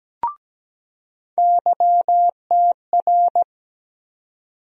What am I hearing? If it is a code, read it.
Morse code, heard as a keyed tone, about 17 wpm, 700 Hz. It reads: YTR
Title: USA Call Sign Postfixes - 3 Letters 17wpm